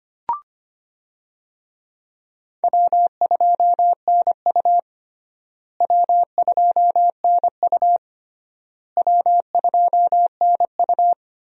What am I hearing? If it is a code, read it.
Morse code, heard as a keyed tone, about 25 wpm, 700 Hz. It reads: W2NU W2NU W2NU
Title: USA Call Signs - 4 Characters - Repeated 3x - 25wpm